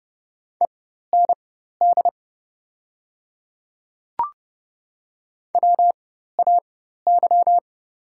Morse code, heard as a keyed tone, about 30 wpm, spaced as for 14 wpm, 700 Hz. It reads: END WAY
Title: 4x Character Spacing - Top 200 Words - 30wpm